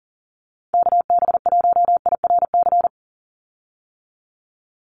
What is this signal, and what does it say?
Morse code, heard as a keyed tone, about 40 wpm, 700 Hz. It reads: KB1IRC
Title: USA Call Signs 40wpm